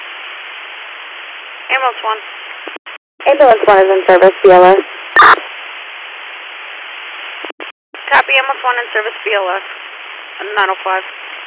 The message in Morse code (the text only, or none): none